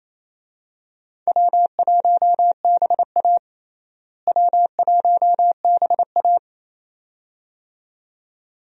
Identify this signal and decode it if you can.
Morse code, heard as a keyed tone, about 28 wpm, 700 Hz. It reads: W1BA W1BA
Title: USA Call Signs - 4 Characters - Repeated 2x - 28wpm